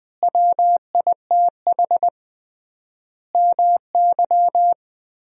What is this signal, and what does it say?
Morse code, heard as a keyed tone, about 21 wpm, 700 Hz. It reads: WITH MY
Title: 3x Word-Spacing - Sentences from Top 300 Words - 20wpm